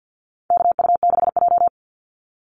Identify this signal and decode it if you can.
Morse code, heard as a keyed tone, about 48 wpm, 700 Hz. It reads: KU6J
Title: NA SOTA Chaser Callsigns 50wpm